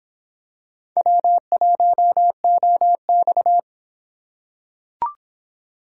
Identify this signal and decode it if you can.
Morse code, heard as a keyed tone, about 26 wpm, 700 Hz. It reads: W1OX